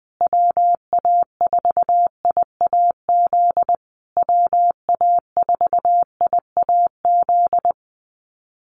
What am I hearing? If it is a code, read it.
Morse code, heard as a keyed tone, about 20 wpm, 700 Hz. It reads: WA4IAZ WA4IAZ
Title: USA Call Signs Repeated 3x 20wpm